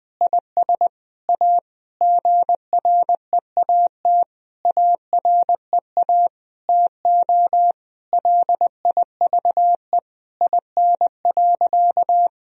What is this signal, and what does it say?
Morse code, heard as a keyed tone, about 20 wpm, 700 Hz. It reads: IS A GREAT AREA TO LIVE IN.
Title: Sentences from Top 1000 Words - Review Part 2 - 20wpm